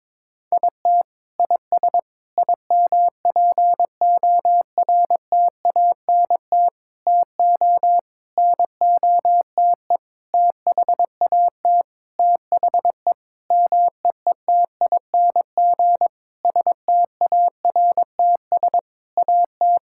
Morse code, heard as a keyed tone, about 22 wpm, 700 Hz. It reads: IT IS IMPORTANT TO NOTE THAT THE MEETING STARTS AT
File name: Sentences with 7-Word Phrases - 22wpm